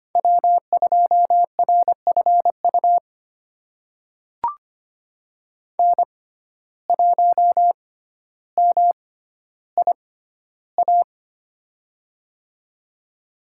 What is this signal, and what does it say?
Morse code, heard as a keyed tone, about 25 wpm, spaced as for 13 wpm, 700 Hz. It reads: W2RFU N1MIA
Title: USA Call Signs - 5 Characters - 6x Character Spacing - 25wpm